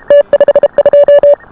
Morse code, heard as a keyed tone, about 28 wpm, 590 Hz. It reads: T52